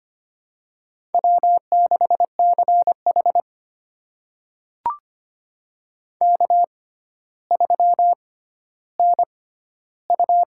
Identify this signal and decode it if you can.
Morse code, heard as a keyed tone, about 25 wpm, spaced as for 13 wpm, 700 Hz. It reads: W6CH K3NU
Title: USA Call Signs - 4 Characters - 6x Character Spacing - 25wpm